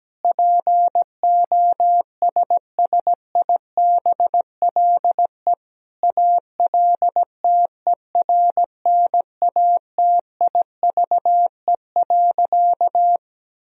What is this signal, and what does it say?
Morse code, heard as a keyed tone, about 17 wpm, 700 Hz. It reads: POSSIBLE ALTERNATIVE.
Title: Sentences from Top 1000 Words - Review Part 4 - 17wpm